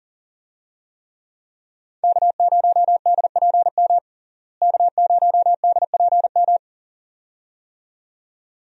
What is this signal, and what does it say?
Morse code, heard as a keyed tone, about 40 wpm, 700 Hz. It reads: K0DPM K0DPM
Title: USA Call Signs - 5 Characters - Repeated 2x - 40wpm